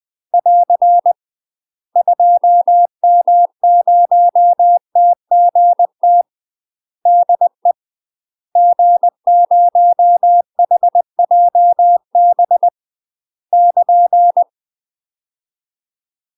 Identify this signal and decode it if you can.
Morse code, heard as a keyed tone, about 20 wpm, 700 Hz. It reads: <AR> 2M0TGT DE G0HJB <KN>
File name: UK MASTER QSOs TRAINING / 20WPM / UK MASTER QSOs 0020